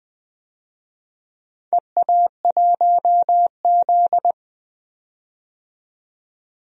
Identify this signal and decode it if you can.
Morse code, heard as a keyed tone, about 20 wpm, 700 Hz. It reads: EA1Z
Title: DXCC-100 Call Signs - 4 Characters - 20wpm